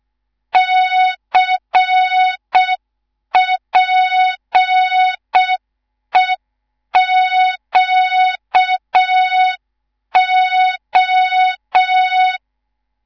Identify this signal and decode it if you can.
Morse code, heard as a keyed tone, about 6 wpm, 3700 Hz. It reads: CPEQO
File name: Morze-HA2MN / hang / 19 GYAKORLAT